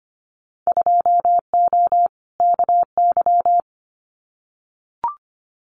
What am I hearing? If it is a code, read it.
Morse code, heard as a keyed tone, about 25 wpm, 700 Hz. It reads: 2O KY
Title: Winter Field Day Exchanges 25wpm